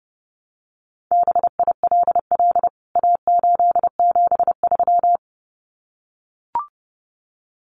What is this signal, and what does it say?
Morse code, heard as a keyed tone, about 30 wpm, 700 Hz. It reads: BILL A873
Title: CWT Exchange 30wpm